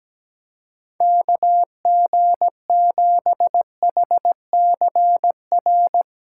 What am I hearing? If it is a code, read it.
Morse code, heard as a keyed tone, about 18 wpm, 700 Hz. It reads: KG7HCR
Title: Speed Racing - USA 2x3 Call Signs 17wpm